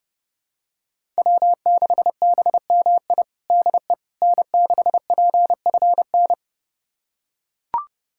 Morse code, heard as a keyed tone, about 30 wpm, 700 Hz. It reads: W6BMI DE N6PFN
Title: USA Callsign DE Callsign - 5 Characters - 30wpm